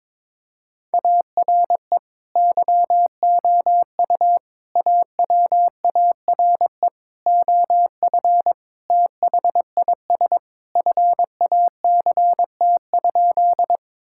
Morse code, heard as a keyed tone, about 22 wpm, 700 Hz. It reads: ARE YOU AWARE OF THIS FACT?
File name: practice / 22wpm / Sentences with 2-Word Phrases - 22wpm